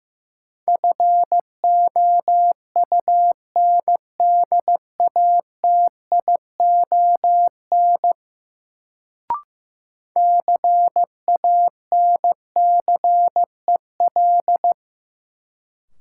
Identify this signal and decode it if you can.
Morse code, heard as a keyed tone, about 15 wpm, 700 Hz. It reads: FOUNDATION CANCEL
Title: Sentences from Top 1200 Words - New Word Drill - 15wpm